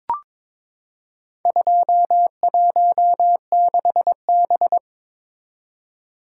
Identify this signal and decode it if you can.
Morse code, heard as a keyed tone, about 22 wpm, 700 Hz. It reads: 216B
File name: practice / 22wpm / Four Letter-Number Combinations - 22wpm